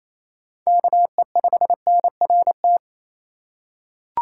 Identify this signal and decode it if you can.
Morse code, heard as a keyed tone, about 28 wpm, 700 Hz. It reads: KE5NRT
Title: USA Call Signs - 2x Character Spacing - 28wpm